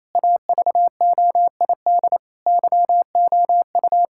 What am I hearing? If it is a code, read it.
Morse code, heard as a keyed tone, about 28 wpm, 690 Hz. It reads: AVOID YOU
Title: Sentences from Top 1000 Words - Review Part 6 - 28wpm